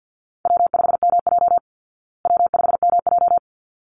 Morse code, heard as a keyed tone, about 48 wpm, 700 Hz. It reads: W5MJ W5MJ
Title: Easy Call Signs Repeated 2x - 50wpm